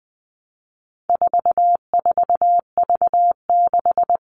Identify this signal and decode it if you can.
Morse code, heard as a keyed tone, about 20 wpm, 700 Hz. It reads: 44V6
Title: All Cut Numbers - 4-Digit 20wpm